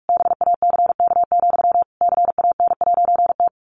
Morse code, heard as a keyed tone, about 45 wpm, 700 Hz. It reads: BACK, CAN'T
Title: A Sound Of Thunder 45wpm